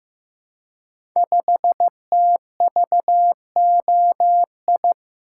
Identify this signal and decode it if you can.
Morse code, heard as a keyed tone, about 15 wpm, 700 Hz. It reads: 5TVOI
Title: Code Groups - 5 Letter:Numbers - 15wpm